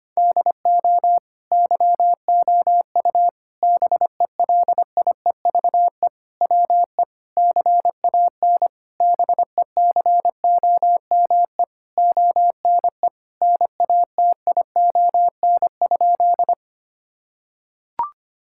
Morse code, heard as a keyed tone, about 25 wpm, 700 Hz. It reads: DO YOU BELIEVE WE CAN BECOME ONE NATION?
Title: Sentences from Top 1000 Words - Review Part 6 - 25wpm